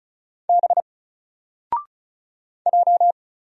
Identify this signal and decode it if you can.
Morse code, heard as a keyed tone, about 36 wpm, 700 Hz. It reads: B J
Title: Single Letter-Number 35wpm